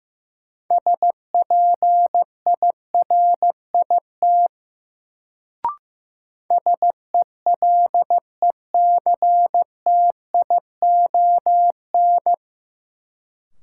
Morse code, heard as a keyed tone, about 15 wpm, 700 Hz. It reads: SPIRIT SELECTION
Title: Sentences from Top 1400 Words - New Word Drill Alt - 15wpm